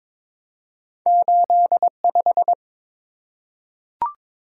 Morse code, heard as a keyed tone, about 23 wpm, 700 Hz. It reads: 85
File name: practice / 22wpm / Two Letter-Number Combinations - 22wpm